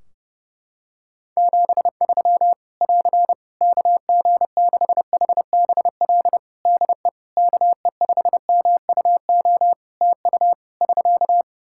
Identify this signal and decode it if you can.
Morse code, heard as a keyed tone, about 32 wpm, 700 Hz. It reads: 73 <AR> KG6HBL DE KE5MUO TU <SK>
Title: Practice QSO - Exchange 4 - Ending - 30wpm